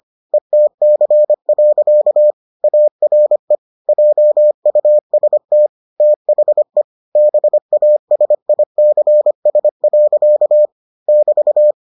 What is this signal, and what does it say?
Morse code, heard as a keyed tone, about 25 wpm, 600 Hz. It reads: ETC. ARE JUST THE BASICS. <BT>